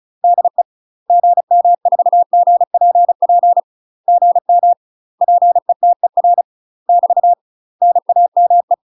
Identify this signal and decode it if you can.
Morse code, heard as a keyed tone, about 35 wpm, 700 Hz. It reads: DE GM4GPP GM PETER <BT> NAME
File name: UK MASTER QSOs TRAINING / 35WPM / UK MASTER QSOs 0029